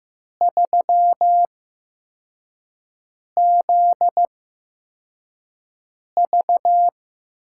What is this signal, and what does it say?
Morse code, heard as a keyed tone, about 15 wpm, spaced as for 4 wpm, 700 Hz. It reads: 3ZV
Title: USA Call Signs - 4 Characters - 8x Character Spacing - 15wpm